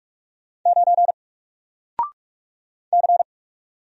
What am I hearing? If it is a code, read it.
Morse code, heard as a keyed tone, about 44 wpm, 700 Hz. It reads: 9 C
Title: Single Character (Letter:Number:Punctuation:Prosign) Alt 45wpm